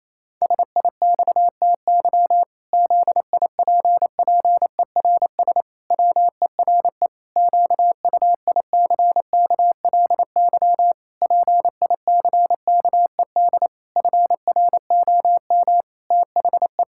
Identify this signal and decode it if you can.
Morse code, heard as a keyed tone, about 28 wpm, 700 Hz. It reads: SIXTY ZIPPERS WERE QUICKLY PICKED FROM THE
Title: Pangram Challenge v2 - 28wpm